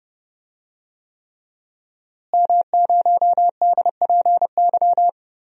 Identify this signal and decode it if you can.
Morse code, heard as a keyed tone, about 30 wpm, 700 Hz. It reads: M0DPY
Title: DXCC-100 Call Signs - 5 Characters - Repeated 3x - 30wpm